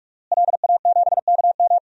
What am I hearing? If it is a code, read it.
Morse code, heard as a keyed tone, about 42 wpm, 700 Hz. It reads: LA7KM
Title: DXCC-100 Call Signs - 5 Characters - Repeated 2x - 45wpm